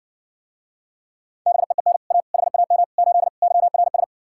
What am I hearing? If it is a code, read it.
Morse code, heard as a keyed tone, about 60 wpm, 700 Hz. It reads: BEEN A HAM 8 YRS